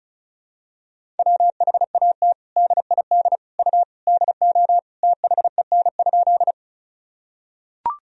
Morse code, heard as a keyed tone, about 36 wpm, 700 Hz. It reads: WHAT DID U DO THEN?
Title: Sentences from Top 100 Words and Abbreviations 35wpm